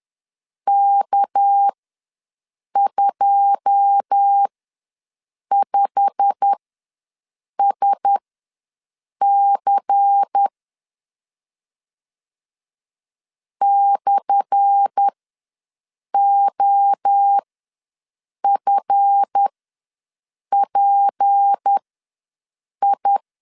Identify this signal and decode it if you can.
Morse code, heard as a keyed tone, about 10 wpm, spaced as for 6 wpm, 800 Hz. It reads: K25SC /OFPI
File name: LEZIONI / 10 SIGNS / 10A SLASH ws 10wpm 4121